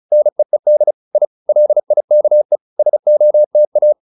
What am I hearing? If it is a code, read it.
Morse code, heard as a keyed tone, about 34 wpm, 600 Hz. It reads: NEED I LIKE SOTA